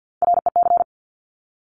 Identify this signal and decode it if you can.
Morse code, heard as a keyed tone, about 50 wpm, 700 Hz. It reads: REC
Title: Top Three Letter Combinations in Words 50wpm